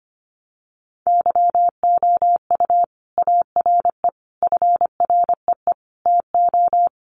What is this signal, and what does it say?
Morse code, heard as a keyed tone, about 25 wpm, 700 Hz. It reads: YOU ARE FREE TO
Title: Sentences from Top 700 Words 25wpm